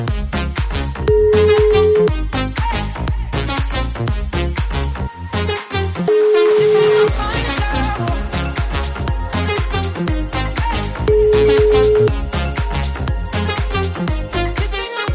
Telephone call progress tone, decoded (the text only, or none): ringback tone